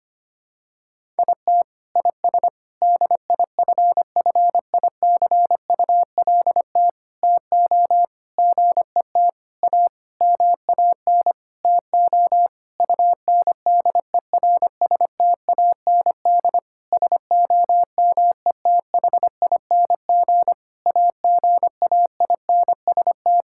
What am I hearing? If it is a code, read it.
Morse code, heard as a keyed tone, about 25 wpm, 700 Hz. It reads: IT IS DIFFICULT TO GET A MAN TO UNDERSTAND SOMETHING AGAINST
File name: practice / 25wpm / Sentences with 9-Word Phrases - 25wpm